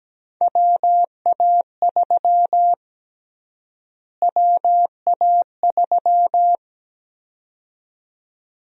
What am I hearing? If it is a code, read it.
Morse code, heard as a keyed tone, about 17 wpm, 700 Hz. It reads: WA3 WA3
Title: USA Call Sign Prefixes - Repeated 2x - 17wpm